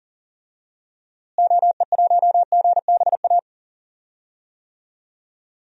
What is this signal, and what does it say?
Morse code, heard as a keyed tone, about 38 wpm, 700 Hz. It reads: OE1GBA
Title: DXCC-100 Call Signs - 6 Characters - Repeated 3x - 40wpm